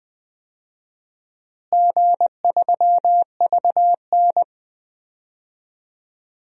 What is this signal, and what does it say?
Morse code, heard as a keyed tone, about 20 wpm, 700 Hz. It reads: G3VN
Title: DXCC-100 Call Signs - 4 Characters - Repeated 2x - 20wpm